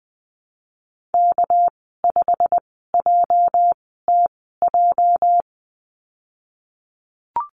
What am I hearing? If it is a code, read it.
Morse code, heard as a keyed tone, about 20 wpm, spaced as for 15 wpm, 700 Hz. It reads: K5JTJ
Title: 207 Lesson 33g - Letter j - Call Signs 20wpm